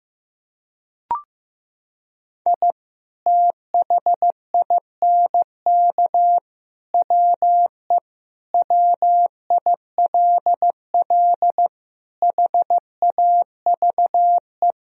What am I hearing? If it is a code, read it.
Morse code, heard as a keyed tone, about 15 wpm, 700 Hz. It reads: I THINK WE WILL HAVE
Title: Sentences from Top 1000 Words - Review Part 1 - 15wpm